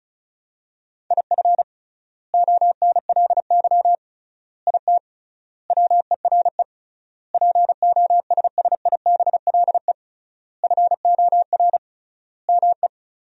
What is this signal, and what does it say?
Morse code, heard as a keyed tone, about 36 wpm, 700 Hz. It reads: IF ONLY IT WERE POSSIBLE FOR ME
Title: 3x Word-Spacing - Sentences from Top 500 Words - 35wpm